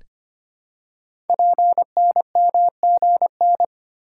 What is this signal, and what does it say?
Morse code, heard as a keyed tone, about 25 wpm, 700 Hz. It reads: PNMGN
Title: Code Groups - 5 Letters - 25wpm